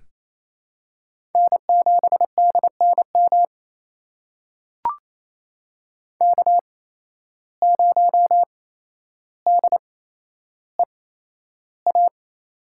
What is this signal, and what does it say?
Morse code, heard as a keyed tone, about 28 wpm, spaced as for 12 wpm, 700 Hz. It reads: N7DNM K0DEA
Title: USA Call Signs - 5 Characters - 8x Character Spacing - 28wpm